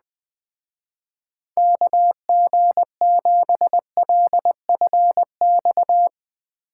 Morse code, heard as a keyed tone, about 20 wpm, 700 Hz. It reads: KG7LFX
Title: Speed Racing - USA 2x3 Call Signs 17wpm